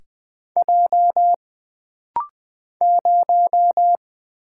Morse code, heard as a keyed tone, about 20 wpm, 700 Hz. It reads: J 0
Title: Single Letter-Number - Rapid-Fire 20wpm